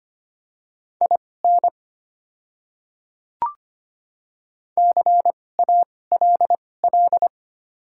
Morse code, heard as a keyed tone, about 25 wpm, spaced as for 18 wpm, 690 Hz. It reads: IN CALL
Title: 2x Character Spacing - Top 100 Words - 25wpm